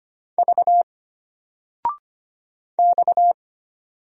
Morse code, heard as a keyed tone, about 25 wpm, 700 Hz. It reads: V X
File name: Single-Letter-Number-25wpm